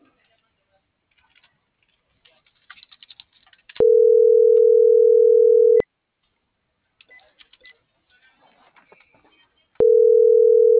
Ringback tone (North America), 440 + 480 Hz.